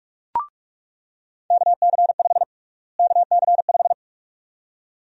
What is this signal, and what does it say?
Morse code, heard as a keyed tone, about 41 wpm, 700 Hz. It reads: KC5 KC5
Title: USA Call Sign Prefixes - Repeated 2x - 45wpm